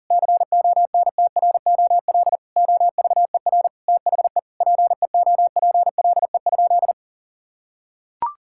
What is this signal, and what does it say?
Morse code, heard as a keyed tone, about 40 wpm, 700 Hz. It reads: CONTROL OVER THE PEOPLE?
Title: Sentences from Top 1000 Words - Review Part 5 - 40wpm